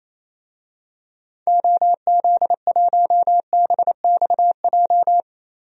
Morse code, heard as a keyed tone, about 28 wpm, 700 Hz. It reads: OZ1BXJ
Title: DXCC-100 Call Signs - 6 Characters - Repeated 3x - 28wpm